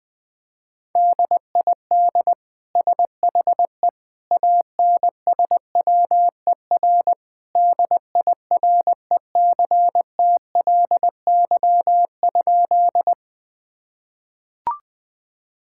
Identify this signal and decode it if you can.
Morse code, heard as a keyed tone, about 21 wpm, 700 Hz. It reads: DID SHE ANSWER DIRECTLY?
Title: Sentences from Top 1000 Words - Review Part 5 - 20wpm